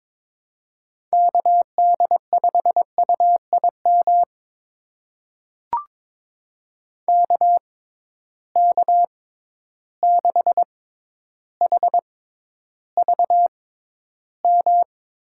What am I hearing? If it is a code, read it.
Morse code, heard as a keyed tone, about 22 wpm, spaced as for 11 wpm, 700 Hz. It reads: KD5UIM KK6HVM
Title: USA Call Signs - 6x Character Spacing - 22wpm